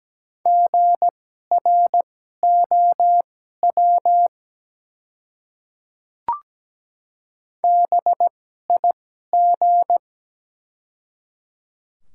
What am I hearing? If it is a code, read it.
Morse code, heard as a keyed tone, about 17 wpm, spaced as for 12 wpm, 700 Hz. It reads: GROW BIG
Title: 2x Character Spacing - Top 300 Words - 17wpm